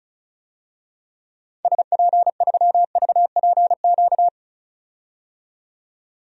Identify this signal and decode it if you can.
Morse code, heard as a keyed tone, about 34 wpm, 700 Hz. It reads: SP3VPQ
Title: DXCC-100 Call Signs - 6 Characters - 35wpm